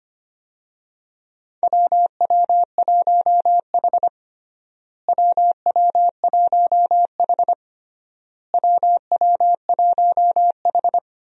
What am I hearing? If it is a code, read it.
Morse code, heard as a keyed tone, about 25 wpm, 700 Hz. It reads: WW1H WW1H WW1H
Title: USA Call Signs - 4 Characters - Repeated 3x - 25wpm